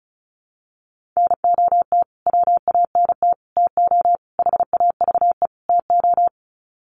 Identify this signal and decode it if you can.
Morse code, heard as a keyed tone, about 34 wpm, 700 Hz. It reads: NOT WANT TO HAVE TO
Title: Sets of 5 Words Encore2 35wpm